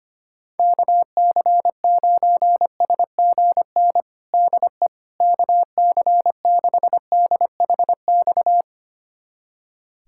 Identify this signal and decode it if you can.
Morse code, heard as a keyed tone, about 25 wpm, 700 Hz. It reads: KC9SGN DE KC6DHX
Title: USA Callsign DE Callsign 25wpm